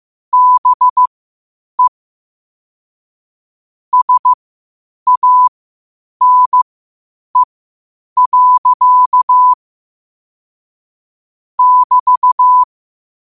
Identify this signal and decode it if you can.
Morse code, heard as a keyed tone, about 15 wpm, spaced as for 8 wpm, 1000 Hz. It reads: BE SANE. <BT>
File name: qotdfiles / 8wpm